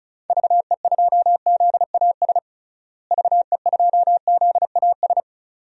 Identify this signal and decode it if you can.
Morse code, heard as a keyed tone, about 36 wpm, 700 Hz. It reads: VE2ZAS VE2ZAS
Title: DXCC-100 Call Signs - 6 Characters - Repeated 2x - 35wpm